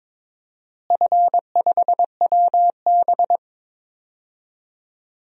Morse code, heard as a keyed tone, about 22 wpm, 700 Hz. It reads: F5WB
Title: DXCC-100 Call Signs - 4 Characters - Repeated 2x - 22wpm